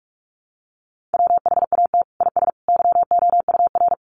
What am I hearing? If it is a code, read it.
Morse code, heard as a keyed tone, about 44 wpm, 700 Hz. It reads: WHAT IS YOUR
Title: Sentences from Top 1500 Words 45wpm